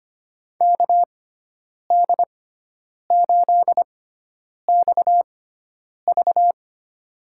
Morse code, heard as a keyed tone, about 25 wpm, spaced as for 9 wpm, 700 Hz. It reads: KD8XV
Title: USA Call Signs - 6x Character Spacing - 25wpm